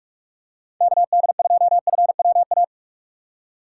Morse code, heard as a keyed tone, about 45 wpm, 700 Hz. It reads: KD2FWA
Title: USA Call Signs - 6 Characters - 6x Character Spacing - 45wpm